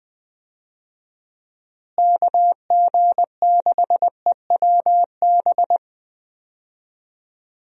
Morse code, heard as a keyed tone, about 21 wpm, 700 Hz. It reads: KG6EWB
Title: USA Call Signs - 6 Characters - Repeated 3x - 20wpm